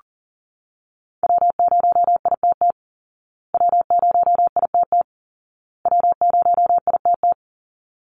Morse code, heard as a keyed tone, about 40 wpm, 700 Hz. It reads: W0ITT W0ITT W0ITT
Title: Call Signs Repeated 2x - 40wpm